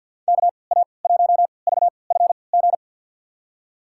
Morse code, heard as a keyed tone, about 51 wpm, spaced as for 28 wpm, 700 Hz. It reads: KA1VFG
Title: USA Call Signs - 3x Character Spacing - 50wpm